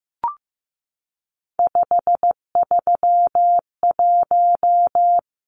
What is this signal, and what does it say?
Morse code, heard as a keyed tone, about 15 wpm, 700 Hz. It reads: 531
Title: Numbers - 3-Digit 15wpm